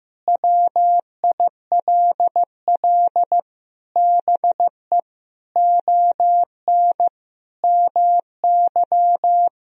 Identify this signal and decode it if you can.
Morse code, heard as a keyed tone, about 15 wpm, 700 Hz. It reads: WILL BE ON MY
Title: Sentences from Top 1000 Words - Review Part 1 - 15wpm